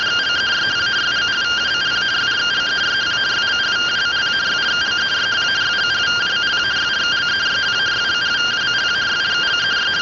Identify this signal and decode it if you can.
Two-tone teleprinter signal, about 170 Hz shift.